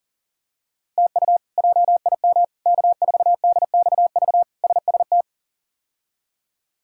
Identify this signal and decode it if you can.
Morse code, heard as a keyed tone, about 40 wpm, 700 Hz. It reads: TU JIM K4DXV SST
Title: K1USN SST QSO Practice - 40wpm